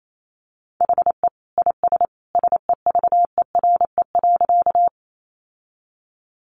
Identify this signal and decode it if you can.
Morse code, heard as a keyed tone, about 28 wpm, 700 Hz. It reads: HE IS SEVERE.
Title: Sentences from Top 1400 Words 28wpm